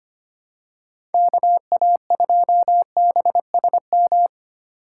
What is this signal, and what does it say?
Morse code, heard as a keyed tone, about 25 wpm, 700 Hz. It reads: KA2BSM
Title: USA Call Signs - 3x Character Spacing - 25wpm